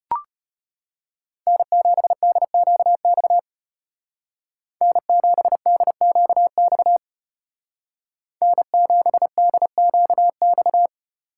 Morse code, heard as a keyed tone, about 30 wpm, 700 Hz. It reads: N7DQX N7DQX N7DQX
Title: Speed Racing - USA 1x3 Call Signs 25wpm